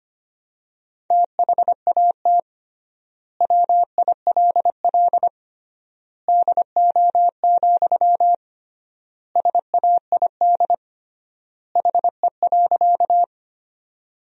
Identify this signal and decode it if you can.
Morse code, heard as a keyed tone, about 25 wpm, 700 Hz. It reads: THAT WILL DO, SAID HE.
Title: 3x Word-Spacing - Sentences from Top 100 Words - 25wpm